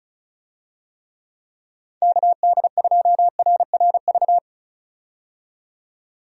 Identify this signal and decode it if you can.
Morse code, heard as a keyed tone, about 36 wpm, 700 Hz. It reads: KD2RRV
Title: USA Call Signs - 6 Characters - 35wpm